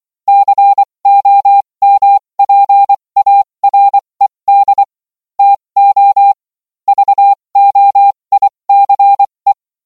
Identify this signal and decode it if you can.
Morse code, heard as a keyed tone, about 24 wpm, spaced as for 20 wpm, 800 Hz. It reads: COMPARED TO VOICE